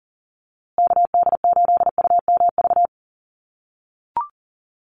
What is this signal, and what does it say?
Morse code, heard as a keyed tone, about 40 wpm, 700 Hz. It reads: KD8UMV
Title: USA Call Signs Repeated 2x 40wpm